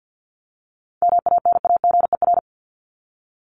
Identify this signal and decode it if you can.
Morse code, heard as a keyed tone, about 50 wpm, 700 Hz. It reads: MANAGER
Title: Top 1501-1750 Words 50wpm